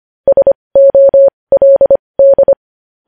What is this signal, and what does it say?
Morse code, heard as a keyed tone, about 25 wpm, spaced as for 20 wpm, 550 Hz. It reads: SOLD
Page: News headlines in Morse code at 20 wpm